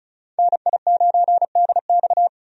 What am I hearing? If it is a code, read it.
Morse code, heard as a keyed tone, about 36 wpm, 700 Hz. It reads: NI9DX
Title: USA Call Signs - 5 Characters - 35wpm